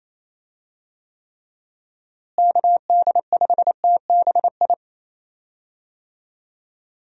Morse code, heard as a keyed tone, about 28 wpm, 700 Hz. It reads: KD5TBI
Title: USA Call Signs - 6 Characters - Repeated 2x - 28wpm